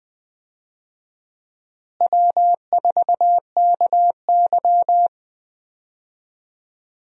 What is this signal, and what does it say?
Morse code, heard as a keyed tone, about 20 wpm, 700 Hz. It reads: W4KY
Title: USA Call Signs - 4 Characters - Repeated 2x - 20wpm